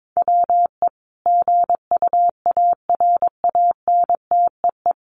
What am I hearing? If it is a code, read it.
Morse code, heard as a keyed tone, about 22 wpm, 700 Hz. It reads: WE GUARANTEE
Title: A Sound Of Thunder 22wpm